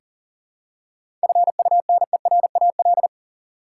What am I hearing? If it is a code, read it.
Morse code, heard as a keyed tone, about 38 wpm, 700 Hz. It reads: FUNERAL
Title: Seven Letter Words - 40wpm